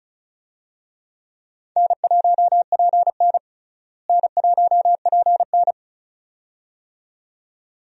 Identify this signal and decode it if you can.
Morse code, heard as a keyed tone, about 35 wpm, 700 Hz. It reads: N1PN N1PN
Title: USA Call Signs - 4 Characters - Repeated 2x - 35wpm